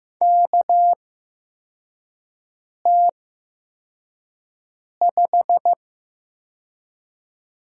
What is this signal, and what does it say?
Morse code, heard as a keyed tone, about 15 wpm, spaced as for 4 wpm, 700 Hz. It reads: KT5
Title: USA Call Signs - 4 Characters - 8x Character Spacing - 15wpm